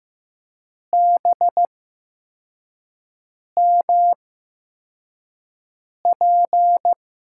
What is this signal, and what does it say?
Morse code, heard as a keyed tone, about 15 wpm, spaced as for 4 wpm, 700 Hz. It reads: BMP
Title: USA Call Signs - 6 Characters - 8x Character Spacing - 15wpm